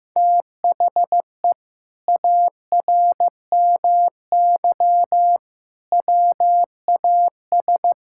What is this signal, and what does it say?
Morse code, heard as a keyed tone, about 15 wpm, 700 Hz. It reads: THE ARMY WAS
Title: Sentences from Top 1600 Words 15wpm